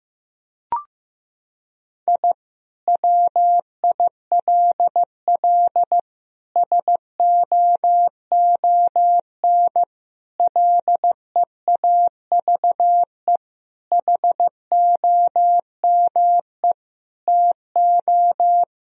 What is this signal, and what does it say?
Morse code, heard as a keyed tone, about 15 wpm, 700 Hz. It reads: I WILL SOON LEAVE HOME TO
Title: Sentences from Top 1000 Words - Review Part 2 - 15wpm